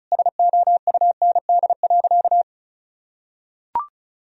Morse code, heard as a keyed tone, about 36 wpm, 700 Hz. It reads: SOUND.
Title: Sentences from Top 1000 Words - Review Part 1 - 35wpm